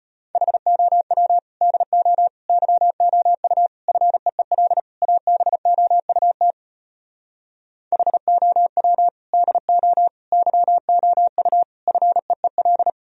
Morse code, heard as a keyed tone, about 33 wpm, 700 Hz. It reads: HOW DO YOU FEEL ABOUT HOW DO YOU FEEL